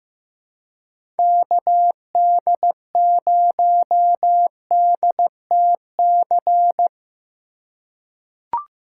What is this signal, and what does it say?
Morse code, heard as a keyed tone, about 15 wpm, 700 Hz. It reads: KD0DTC
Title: USA Call Signs - 6 Characters - 4x Character Spacing - 15wpm